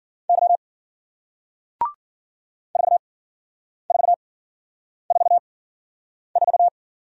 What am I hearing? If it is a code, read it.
Morse code, heard as a keyed tone, about 50 wpm, spaced as for 9 wpm, 700 Hz. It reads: Y 4444